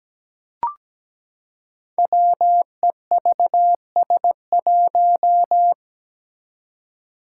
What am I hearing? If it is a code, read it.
Morse code, heard as a keyed tone, about 17 wpm, 700 Hz. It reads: WEVS1
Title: Five Challenging Letter-Number Combinations 17wpm